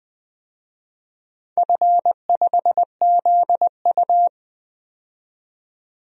Morse code, heard as a keyed tone, about 20 wpm, 700 Hz. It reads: F5ZU
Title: DXCC-100 Call Signs - 4 Characters - 20wpm